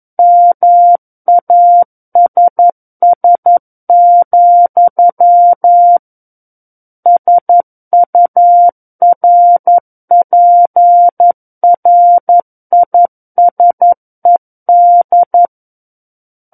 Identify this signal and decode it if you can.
Morse code, heard as a keyed tone, about 11 wpm, 700 Hz. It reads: MASS, SURPRISED
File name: Book One / 02-Chapter 2 - 11 WPM